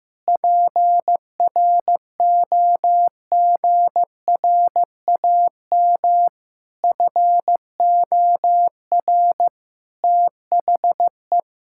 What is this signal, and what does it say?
Morse code, heard as a keyed tone, about 15 wpm, 700 Hz. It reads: PROGRAM FOR THE
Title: Sentences from Top 1000 Words - Review Part 5 - 15wpm